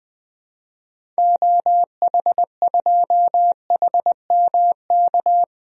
Morse code, heard as a keyed tone, about 20 wpm, 700 Hz. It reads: OH2HMK
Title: DXCC-100 Call Signs - 6 Characters - Repeated 3x - 20wpm